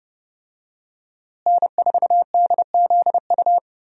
Morse code, heard as a keyed tone, about 30 wpm, 700 Hz. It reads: N4DZU
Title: USA Call Signs - 5 Characters - Repeated 2x - 30wpm